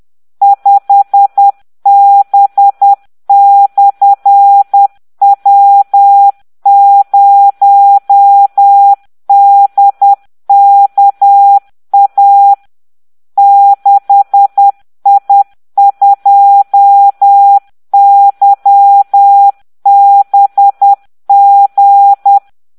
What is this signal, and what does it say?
Morse code, heard as a keyed tone, about 10 wpm, 800 Hz. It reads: 5B/W0DKA 6I2YBG